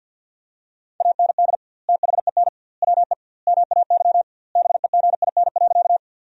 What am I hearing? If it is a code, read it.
Morse code, heard as a keyed tone, about 50 wpm, 700 Hz. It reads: AND THEN WE MAY BEGIN.
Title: 2x Word-Spacing - Sentences from Top 300 Words - 50wpm